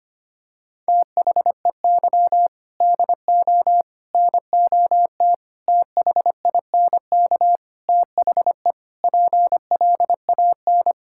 Morse code, heard as a keyed tone, about 25 wpm, 700 Hz. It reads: THEY DO NOT THINK THE PLAN
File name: practice / 25wpm / Sentences with 4-Word Phrases - 25wpm